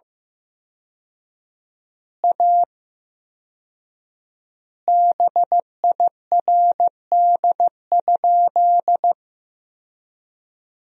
Morse code, heard as a keyed tone, about 15 wpm, 700 Hz. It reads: A BIRD?
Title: 4x Word-Spacing - Sentences from Top 400 Words - 15wpm